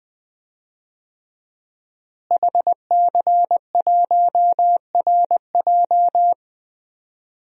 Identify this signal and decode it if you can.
Morse code, heard as a keyed tone, about 20 wpm, 700 Hz. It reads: HC1RJ
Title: DXCC-100 Call Signs - 5 Characters - Repeated 3x - 20wpm